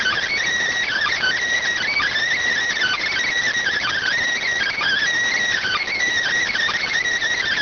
Emergency Alert System (SAME) tones (header bursts), heard as harsh buzzing bursts. It is heard from 0.4 to 7.0 s.